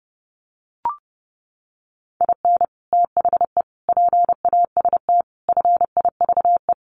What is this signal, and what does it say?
Morse code, heard as a keyed tone, about 30 wpm, 700 Hz. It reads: IN THE PAST FIVE
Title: Sets of 5 Words Encore 30wpm